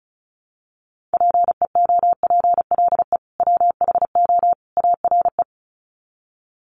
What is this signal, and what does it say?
Morse code, heard as a keyed tone, about 34 wpm, 700 Hz. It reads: PEOPLE WHO ARE
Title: Sets of 3 Words 35wpm